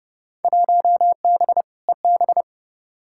Morse code, heard as a keyed tone, about 30 wpm, 700 Hz. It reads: 1B EB
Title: Field Day Exchanges - 30wpm